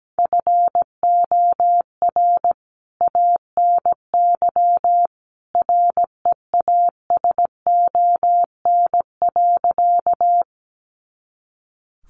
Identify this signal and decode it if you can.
Morse code, heard as a keyed tone, about 17 wpm, 700 Hz. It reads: FOR ANY REASON.
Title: A Sound Of Thunder 17wpm